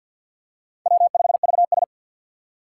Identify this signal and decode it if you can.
Morse code, heard as a keyed tone, about 52 wpm, 700 Hz. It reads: W5VS
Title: USA Call Signs - 3x Character Spacing - 50wpm